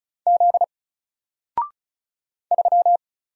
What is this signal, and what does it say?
Morse code, heard as a keyed tone, about 36 wpm, 700 Hz. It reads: Z 3
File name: Single-Letter-Number-35wpm